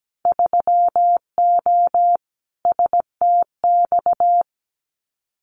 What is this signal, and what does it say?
Morse code, heard as a keyed tone, about 17 wpm, 700 Hz. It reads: 3O STX
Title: Winter Field Day Exchanges 17wpm